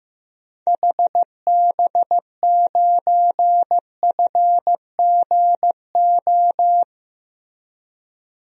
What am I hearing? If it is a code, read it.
Morse code, heard as a keyed tone, about 15 wpm, 700 Hz. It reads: HB9FGO
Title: DXCC-100 Call Signs - 6 Characters - Repeated 2x - 15wpm